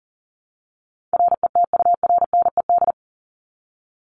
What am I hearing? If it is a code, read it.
Morse code, heard as a keyed tone, about 40 wpm, 700 Hz. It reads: RETURNED
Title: ED Suffix - Top 100 Words with Base Words 40wpm